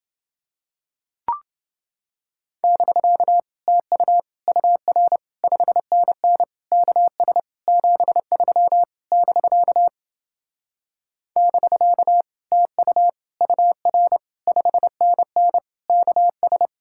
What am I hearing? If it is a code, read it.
Morse code, heard as a keyed tone, about 28 wpm, 700 Hz. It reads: <BK> TU UR 5NN KS 73 <BK> <BK> TU UR 5NN KS